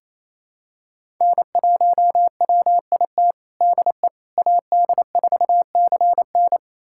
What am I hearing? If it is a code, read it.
Morse code, heard as a keyed tone, about 28 wpm, 700 Hz. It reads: N1WIT DE AD4CN
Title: USA Callsign DE Callsign - 5 Characters - 28wpm